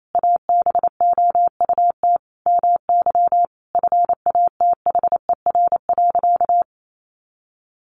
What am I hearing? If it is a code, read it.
Morse code, heard as a keyed tone, about 28 wpm, 700 Hz. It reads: ABOUT MY FATHER.
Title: Sentences from Top 400 Words 28wpm